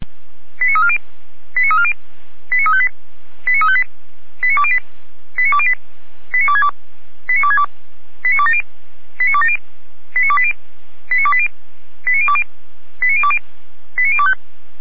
Five-tone selective call (ZVEI), heard as quick runs of five tones, tones of about 70 ms. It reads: 89340 89340 89358 89358 89208 89208 89262 89262 89270 89270 89280 89280 89020 89020 89025